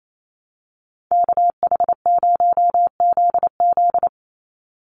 Morse code, heard as a keyed tone, about 28 wpm, 700 Hz. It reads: KH0ZZ
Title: Call Signs - 28wpm